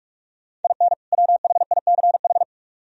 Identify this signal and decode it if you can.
Morse code, heard as a keyed tone, about 62 wpm, spaced as for 53 wpm, 700 Hz. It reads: IN WHICH
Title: Speed-Racing - Sets of 2 Words 45wpm